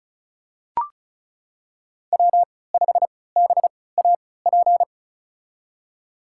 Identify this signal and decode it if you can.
Morse code, heard as a keyed tone, about 34 wpm, spaced as for 20 wpm, 700 Hz. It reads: W5BAP
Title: USA Call Signs - 5 Characters - 3x Character Spacing - 35wpm